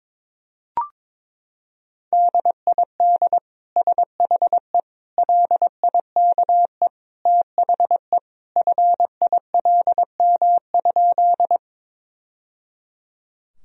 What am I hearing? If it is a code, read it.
Morse code, heard as a keyed tone, about 22 wpm, 700 Hz. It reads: DID SHE LIKE THE FILM?
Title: Sentences from Top 1000 Words - Review Part 3 - 22wpm